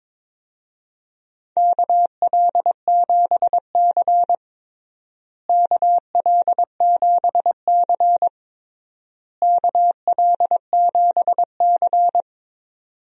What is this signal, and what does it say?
Morse code, heard as a keyed tone, about 22 wpm, 700 Hz. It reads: KL7C KL7C KL7C
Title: DXCC-100 Call Signs - 4 Characters - Repeated 3x - 22wpm